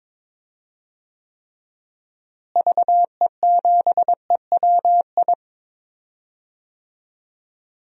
Morse code, heard as a keyed tone, about 22 wpm, 700 Hz. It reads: VE7EWI
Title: DXCC-100 Call Signs - 6 Characters - Repeated 2x - 22wpm